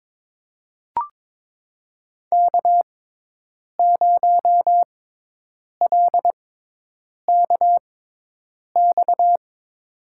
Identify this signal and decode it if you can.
Morse code, heard as a keyed tone, about 22 wpm, spaced as for 8 wpm, 700 Hz. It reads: K0LKX
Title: USA Call Signs - 5 Characters - 6x Character Spacing - 22wpm